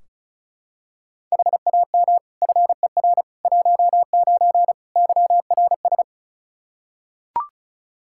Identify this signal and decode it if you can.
Morse code, heard as a keyed tone, about 36 wpm, 700 Hz. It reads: HAM FER 19 YRS
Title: K1USN SST QSO Practice 35wpm